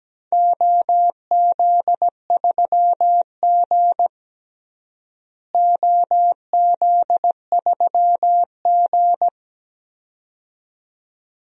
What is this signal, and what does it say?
Morse code, heard as a keyed tone, about 17 wpm, 700 Hz. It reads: OZ3G OZ3G
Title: DXCC-100 Call Signs - 4 Characters - Repeated 2x - 17wpm